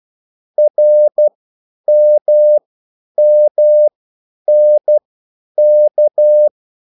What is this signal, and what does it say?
Morse code, heard as a keyed tone, about 12 wpm, spaced as for 9 wpm, 600 Hz. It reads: RMMNK
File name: QSL / CW lesson